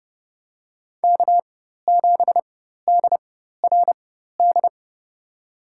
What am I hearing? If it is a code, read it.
Morse code, heard as a keyed tone, about 31 wpm, spaced as for 14 wpm, 700 Hz. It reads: K7DRD
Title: USA Call Signs - 5 Characters - 4x Character Spacing - 30wpm